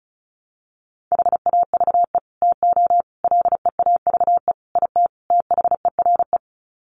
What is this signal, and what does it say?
Morse code, heard as a keyed tone, about 34 wpm, 700 Hz. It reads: HAVE TO LEAVE IT THERE
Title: Sets of 5 Words Encore2 35wpm